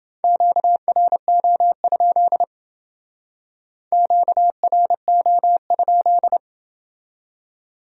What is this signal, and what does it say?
Morse code, heard as a keyed tone, about 28 wpm, 700 Hz. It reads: QRO? QRO?